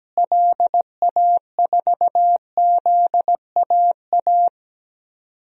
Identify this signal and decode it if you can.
Morse code, heard as a keyed tone, about 17 wpm, 700 Hz. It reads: LA4ZAA
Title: DXCC-100 Call Signs - 6 Characters - Repeated 2x - 17wpm